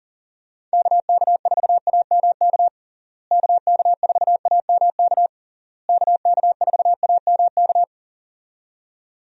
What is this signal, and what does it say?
Morse code, heard as a keyed tone, about 40 wpm, 700 Hz. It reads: KK4AMK KK4AMK KK4AMK
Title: USA Call Signs - 6 Characters - Repeated 3x - 40wpm